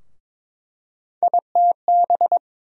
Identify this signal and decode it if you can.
Morse code, heard as a keyed tone, about 22 wpm, 700 Hz. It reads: ITB
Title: USA Call Sign Postfixes - 3 Letters 22wpm